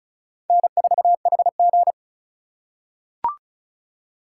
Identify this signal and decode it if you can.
Morse code, heard as a keyed tone, about 36 wpm, 700 Hz. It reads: N4HG
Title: USA Call Signs - 4 Characters - 3x Character Spacing - 35wpm